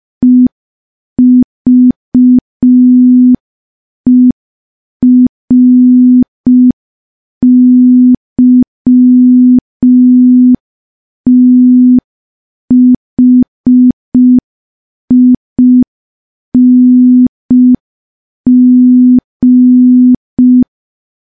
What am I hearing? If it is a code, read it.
Morse code, heard as a keyed tone, about 5 wpm, 250 Hz. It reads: EVERYTHING
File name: morse / morse-training-05WPM